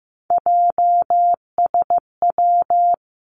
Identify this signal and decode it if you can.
Morse code, heard as a keyed tone, about 15 wpm, 700 Hz. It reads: JSW
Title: Three Challenging Letter-Number Combinations 15wpm